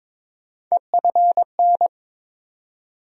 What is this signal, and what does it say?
Morse code, heard as a keyed tone, about 22 wpm, 700 Hz. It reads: EFN
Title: USA Call Sign Postfixes - 3 Letters 22wpm